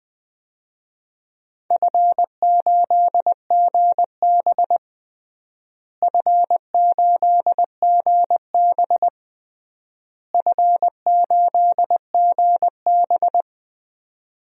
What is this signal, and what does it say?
Morse code, heard as a keyed tone, about 20 wpm, 700 Hz. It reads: F8GB F8GB F8GB
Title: DXCC-100 Call Signs - 4 Characters - Repeated 3x - 20wpm